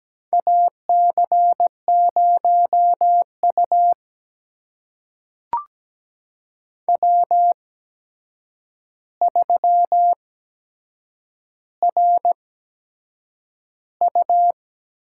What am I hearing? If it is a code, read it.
Morse code, heard as a keyed tone, about 17 wpm, spaced as for 7 wpm, 700 Hz. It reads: AC0U W3RU
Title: USA Call Signs - 8x Character Spacing - 17wpm